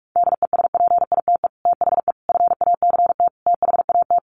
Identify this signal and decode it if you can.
Morse code, heard as a keyed tone, about 44 wpm, 700 Hz. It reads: DESPITE THE FACT THAT
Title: Sets of 4 Words 45wpm